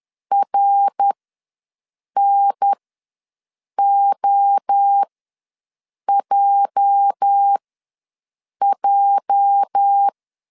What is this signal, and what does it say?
Morse code, heard as a keyed tone, about 10 wpm, spaced as for 6 wpm, 800 Hz. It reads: RNOJJ
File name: LEZIONI / 8 YJ45 / 8 YJ ws 10wpm 4043